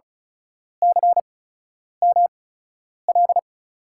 Morse code, heard as a keyed tone, about 36 wpm, spaced as for 10 wpm, 700 Hz. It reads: CML
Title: USA Call Signs - 5 Characters - 8x Character Spacing - 35wpm